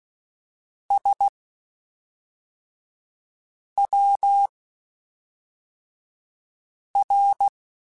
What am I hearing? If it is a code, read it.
Morse code, heard as a keyed tone, about 16 wpm, spaced as for 3 wpm, 800 Hz. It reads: SWR
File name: (spaced) / QSOs Spaced-mp3s / WD0FBT 1 @5 sp